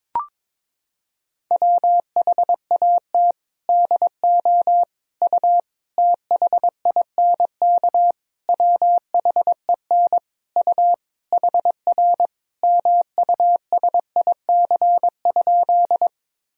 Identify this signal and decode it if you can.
Morse code, heard as a keyed tone, about 22 wpm, 700 Hz. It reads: WHAT DO U THINK WHEN U HR MUSIC?
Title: Sentences from Top 300 Words and Abbreviations 22wpm